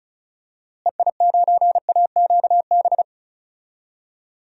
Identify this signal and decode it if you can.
Morse code, heard as a keyed tone, about 36 wpm, 700 Hz. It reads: EI9AQB
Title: DXCC-100 Call Signs - 6 Characters - Repeated 2x - 35wpm